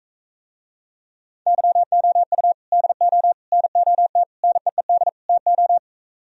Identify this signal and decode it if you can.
Morse code, heard as a keyed tone, about 45 wpm, 700 Hz. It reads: YOU DO NOT NEED TO